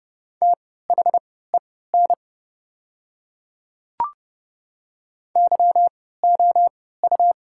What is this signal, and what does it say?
Morse code, heard as a keyed tone, about 30 wpm, spaced as for 17 wpm, 700 Hz. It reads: THEN YOU
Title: 3x Character Spacing - Top 100 Words - 30wpm